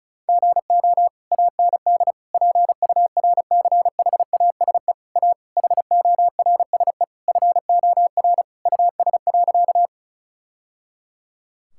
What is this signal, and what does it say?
Morse code, heard as a keyed tone, about 36 wpm, 700 Hz. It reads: GO AND PURCHASE A HORSE FOR US.
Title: Sentences from Top 1000 Words - Review Part 5 - 35wpm